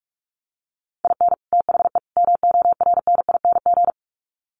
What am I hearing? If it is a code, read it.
Morse code, heard as a keyed tone, about 44 wpm, 700 Hz. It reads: IN THE MORNING
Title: Sets of 3 Words 45wpm